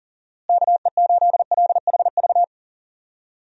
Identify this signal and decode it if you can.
Morse code, heard as a keyed tone, about 40 wpm, 700 Hz. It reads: KE8LHV
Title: USA Call Signs - 6 Characters - Repeated 3x - 40wpm